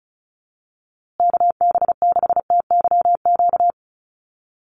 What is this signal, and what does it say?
Morse code, heard as a keyed tone, about 34 wpm, 700 Hz. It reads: KB6TYQ
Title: USA Call Signs Repeated 2x 35wpm